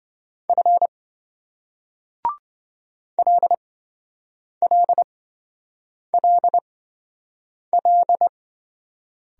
Morse code, heard as a keyed tone, about 28 wpm, spaced as for 7 wpm, 700 Hz. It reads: F LLLL